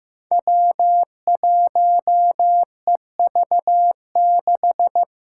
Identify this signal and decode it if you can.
Morse code, heard as a keyed tone, about 15 wpm, 700 Hz. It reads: W1EV6
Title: Five Challenging Letter-Number Combinations 15wpm